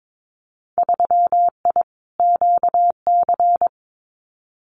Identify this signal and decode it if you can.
Morse code, heard as a keyed tone, about 22 wpm, 700 Hz. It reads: 3I QC
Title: Winter Field Day Exchanges Encore 22wpm